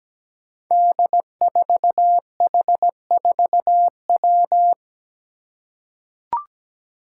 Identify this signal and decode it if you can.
Morse code, heard as a keyed tone, about 17 wpm, 700 Hz. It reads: D4H4W
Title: Five Challenging Letter-Number Combinations 17wpm